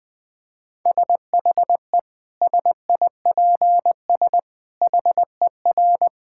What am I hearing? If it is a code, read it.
Morse code, heard as a keyed tone, about 20 wpm, 700 Hz. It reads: SHE SIPS HER
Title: Difficult Sentences with Hints - 20wpm